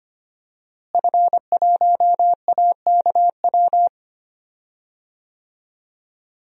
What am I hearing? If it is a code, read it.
Morse code, heard as a keyed tone, about 25 wpm, 700 Hz. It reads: F1AKW
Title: DXCC-100 Call Signs - 5 Characters - Repeated 3x - 25wpm